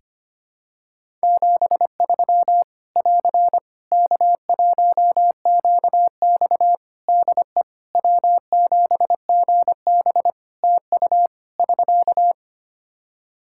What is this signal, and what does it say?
Morse code, heard as a keyed tone, about 25 wpm, 700 Hz. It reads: 73 <AR> K1QX DE W7GB TU <SK>
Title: K1USN SST QSO Practice 25wpm